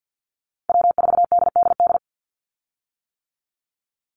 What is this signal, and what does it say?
Morse code, heard as a keyed tone, about 50 wpm, 700 Hz. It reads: W4DDD
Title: Call Signs - 50wpm